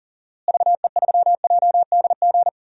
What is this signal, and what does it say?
Morse code, heard as a keyed tone, about 42 wpm, 700 Hz. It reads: VE3JDG
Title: DXCC-100 Call Signs - 6 Characters - Repeated 3x - 40wpm